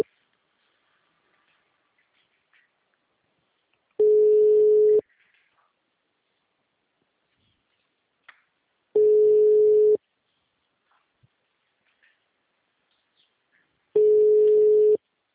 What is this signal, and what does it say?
Ringback tone (Europe, 425 Hz).